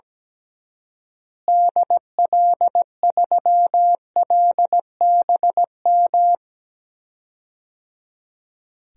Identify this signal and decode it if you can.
Morse code, heard as a keyed tone, about 17 wpm, 700 Hz. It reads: DL3LBM
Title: DXCC-100 Call Signs - 6 Characters - Repeated 3x - 17wpm